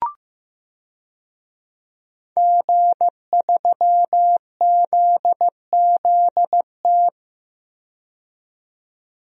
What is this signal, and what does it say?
Morse code, heard as a keyed tone, about 15 wpm, 700 Hz. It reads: G3ZZT
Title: DXCC-100 Call Signs - 5 Characters - 15wpm